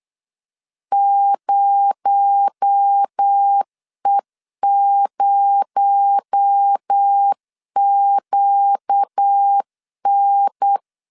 Morse code, heard as a keyed tone, about 8 wpm, 800 Hz. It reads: 0E0QN
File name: LEZIONI / 6 PKX0 / 6 PKX0 8wpm 3341